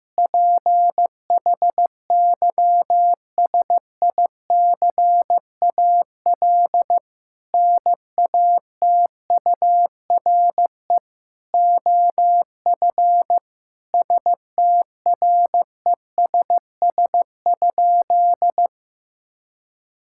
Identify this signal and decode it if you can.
Morse code, heard as a keyed tone, about 15 wpm, 700 Hz. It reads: PHYSICAL NATURE OF STRESS?
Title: Sentences from Top 1000 Words - Review Part 6 - 15wpm